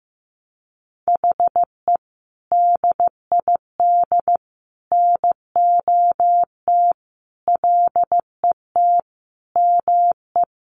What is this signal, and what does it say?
Morse code, heard as a keyed tone, about 15 wpm, 700 Hz. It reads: HE DID NOT LET ME
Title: Sentences from Top 1200 Words 15wpm